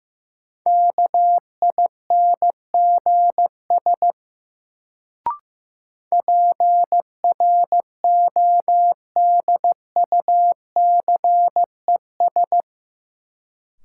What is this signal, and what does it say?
Morse code, heard as a keyed tone, about 15 wpm, 700 Hz. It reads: KINGS PRODUCES
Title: Sentences from Top 1000 Words - Review Part 6 - New Word Drill - 15wpm